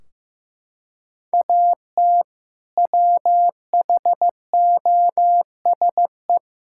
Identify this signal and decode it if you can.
Morse code, heard as a keyed tone, about 16 wpm, 700 Hz. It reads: AT WHOSE
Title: Sentences from Top 1600 Words 15wpm